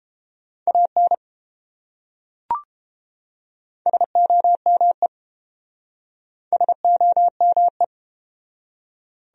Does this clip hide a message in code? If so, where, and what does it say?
Yes: Morse code from 0.7 to 7.8 s, heard as a keyed tone, about 31 wpm, 700 Hz. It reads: AN SOME SOME